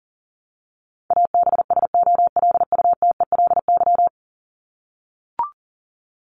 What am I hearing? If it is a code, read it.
Morse code, heard as a keyed tone, about 40 wpm, 700 Hz. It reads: ABSOLUTELY
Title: LY Suffix - Top 100 Words with Base Words 40wpm